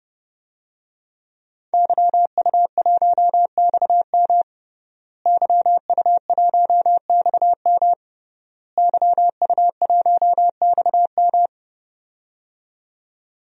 Morse code, heard as a keyed tone, about 31 wpm, 700 Hz. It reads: YU1XM YU1XM YU1XM
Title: DXCC-100 Call Signs - 5 Characters - Repeated 3x - 30wpm